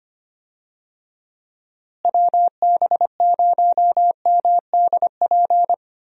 Morse code, heard as a keyed tone, about 25 wpm, 700 Hz. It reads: WB0MDP
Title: USA Call Signs - 6 Characters - Repeated 3x - 25wpm